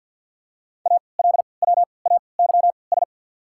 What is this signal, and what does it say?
Morse code, heard as a keyed tone, about 52 wpm, 690 Hz. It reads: A L W A Y S
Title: 3x Character Spacing - Top 300 Words - 50wpm